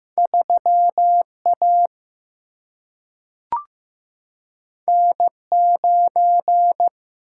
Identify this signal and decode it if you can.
Morse code, heard as a keyed tone, about 15 wpm, 700 Hz. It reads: 3A N9
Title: Single-Letter Single-Number 15wpm